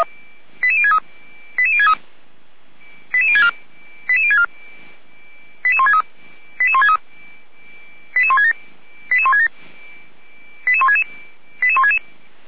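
Five-tone selective call (ZVEI), heard as quick runs of five tones, tones of about 70 ms. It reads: 88063 88063 88064 88064 88163 88167 88167 88166 88166
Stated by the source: ZVEI five-tone selective call 88352 received on SDR.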